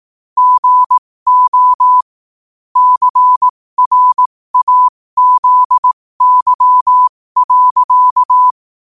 Morse code, heard as a keyed tone, about 18 wpm, spaced as for 15 wpm, 1000 Hz. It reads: GO CRAZY.
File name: qotdfiles / 15wpm